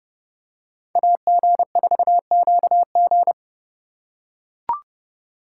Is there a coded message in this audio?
Yes: Morse code from 1.0 to 3.3 s, heard as a keyed tone, about 29 wpm, 700 Hz. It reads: AG4QG